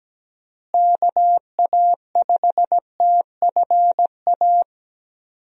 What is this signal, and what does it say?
Morse code, heard as a keyed tone, about 17 wpm, 700 Hz. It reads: KA5TFA
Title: USA Call Signs - 6 Characters - Repeated 2x - 17wpm